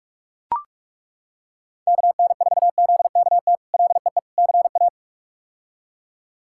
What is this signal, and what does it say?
Morse code, heard as a keyed tone, about 41 wpm, 700 Hz. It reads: KN4ZKT LEE CA
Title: K1USN SST QSO Practice - 45wpm